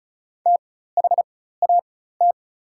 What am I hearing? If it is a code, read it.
Morse code, heard as a keyed tone, about 36 wpm, spaced as for 16 wpm, 700 Hz. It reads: THAT
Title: 4x Character Spacing - Top 100 Words - 35wpm